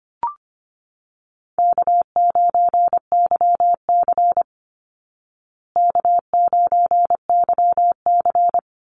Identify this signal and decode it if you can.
Morse code, heard as a keyed tone, about 25 wpm, 700 Hz. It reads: K9YC K9YC
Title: Easy Call Signs Repeated 3x - 25wpm